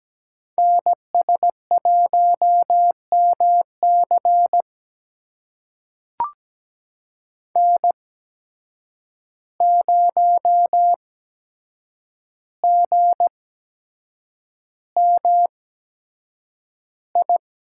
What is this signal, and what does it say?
Morse code, heard as a keyed tone, about 17 wpm, spaced as for 7 wpm, 700 Hz. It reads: NS1MC N0GMI
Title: USA Call Signs - 5 Characters - 8x Character Spacing - 17wpm